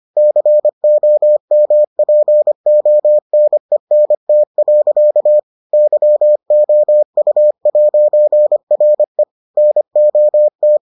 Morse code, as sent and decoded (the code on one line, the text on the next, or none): -.-. --- -- .--. --- -. -. - .-.-.- / -.-- --- ..- .----. .-. / -. --- -
COMPONENT. YOU'RE NOT